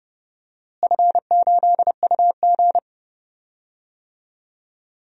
Morse code, heard as a keyed tone, about 29 wpm, 700 Hz. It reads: F8UG
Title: DXCC-100 Call Signs - 4 Characters - Repeated 2x - 30wpm